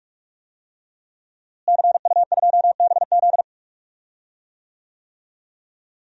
Morse code, heard as a keyed tone, about 46 wpm, 700 Hz. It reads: CU2BZ
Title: DXCC-100 Call Signs - 5 Characters - Repeated 2x - 45wpm